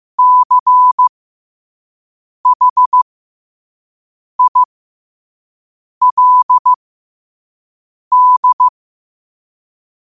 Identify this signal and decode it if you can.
Morse code, heard as a keyed tone, about 15 wpm, spaced as for 5 wpm, 1000 Hz. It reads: CHILD
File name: qotdfiles / 5wpm